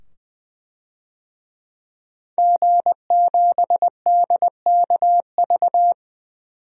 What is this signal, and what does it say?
Morse code, heard as a keyed tone, about 20 wpm, 700 Hz. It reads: G7DKV
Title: DXCC-100 Call Signs - 5 Characters - 20wpm